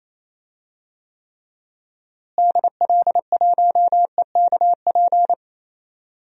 Morse code, heard as a keyed tone, about 27 wpm, 700 Hz. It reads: DL1EKP
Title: DXCC-100 Call Signs - 6 Characters - Repeated 2x - 28wpm